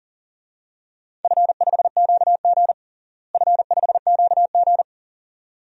Morse code, heard as a keyed tone, about 40 wpm, 700 Hz. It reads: F5QG F5QG
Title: DXCC-100 Call Signs - 4 Characters - Repeated 2x - 40wpm